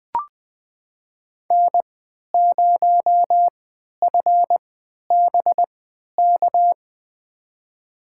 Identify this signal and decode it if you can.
Morse code, heard as a keyed tone, about 20 wpm, spaced as for 11 wpm, 700 Hz. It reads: N0FBK
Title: USA Call Signs - 5 Characters - 3x Character Spacing - 20wpm